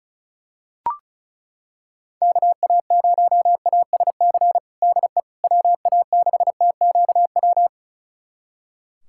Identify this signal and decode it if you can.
Morse code, heard as a keyed tone, about 34 wpm, 700 Hz. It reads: KA0ASC DE WA6TQW
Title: USA Callsign DE Callsign - 6 Characters - 35wpm